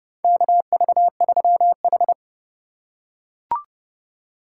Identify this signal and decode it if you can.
Morse code, heard as a keyed tone, about 30 wpm, 700 Hz. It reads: KV3H
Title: USA Call Signs - 3x Character Spacing - 30wpm